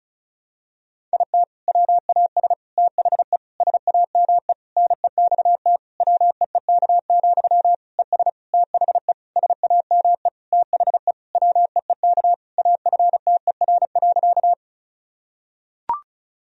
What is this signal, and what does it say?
Morse code, heard as a keyed tone, about 34 wpm, 700 Hz. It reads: IT WAS THE SAME NEXT WEEK, ES THE SAME THE WEEK AFTER.
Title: Sentences from Top 500 Words and Abbreviations 35wpm